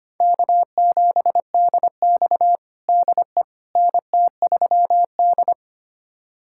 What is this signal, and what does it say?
Morse code, heard as a keyed tone, about 25 wpm, 700 Hz. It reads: K7DX DE NT3D
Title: USA Callsign DE Callsign - 4 Characters - 25wpm